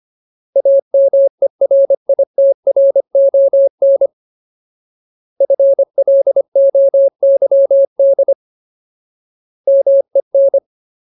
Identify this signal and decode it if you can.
Morse code, heard as a keyed tone, about 25 wpm, 550 Hz. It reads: AMERITRON FLOYD MEN
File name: Ultimate Word List / 25WPM Ultimate